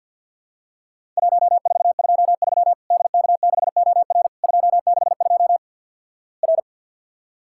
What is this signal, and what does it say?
Morse code, heard as a keyed tone, about 50 wpm, 700 Hz. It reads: 1423 DK6OR 262 R
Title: Worked All Europe DX Contest - QTC Exchange with Repeats - 50wpm